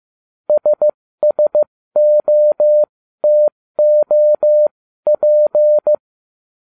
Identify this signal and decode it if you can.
Morse code, heard as a keyed tone, about 15 wpm, spaced as for 13 wpm, 600 Hz. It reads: SSOTOP